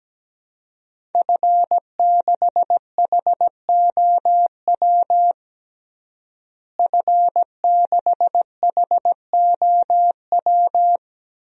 Morse code, heard as a keyed tone, about 17 wpm, 700 Hz. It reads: F6HOW F6HOW
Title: DXCC-100 Call Signs - 5 Characters - Repeated 2x - 17wpm